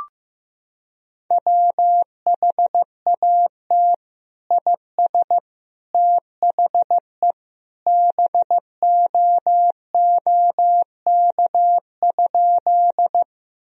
Morse code, heard as a keyed tone, about 15 wpm, 700 Hz. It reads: WHAT IS THE BOOK?
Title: Sentences from Top 1000 Words - Review Part 1 - 15wpm